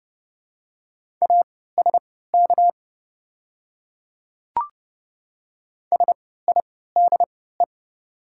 Morse code, heard as a keyed tone, about 30 wpm, spaced as for 17 wpm, 700 Hz. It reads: ASK SIDE
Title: 3x Character Spacing - Top 300 Words - 30wpm